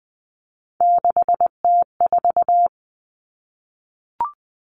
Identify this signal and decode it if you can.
Morse code, heard as a keyed tone, about 20 wpm, 700 Hz. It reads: 6T4
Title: Three Challenging Letter-Number Combinations 20wpm